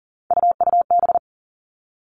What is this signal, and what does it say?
Morse code, heard as a keyed tone, about 40 wpm, 700 Hz. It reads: UUB
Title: Three Challenging Letter-Number Combinations 40wpm